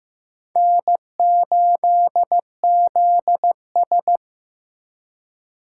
Morse code, heard as a keyed tone, about 18 wpm, 700 Hz. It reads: N8ZS
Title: Speed Racing - USA Call Signs 15wpm